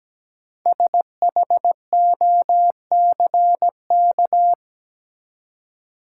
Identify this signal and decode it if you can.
Morse code, heard as a keyed tone, about 17 wpm, 700 Hz. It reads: SHOCK
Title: Five Letter Words - 17wpm